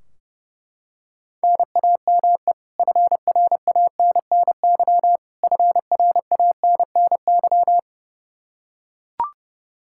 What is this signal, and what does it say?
Morse code, heard as a keyed tone, about 30 wpm, 700 Hz. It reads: NAME FRANNY FRANNY
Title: Practice QSO - Exchange 1 - Beginning - 30wpm